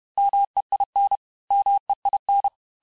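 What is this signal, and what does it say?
Morse code, heard as a keyed tone, about 31 wpm, 800 Hz. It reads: MEIN MEIN
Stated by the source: Morse lesson of words, each word repeated 4 times, at 29 words per minute.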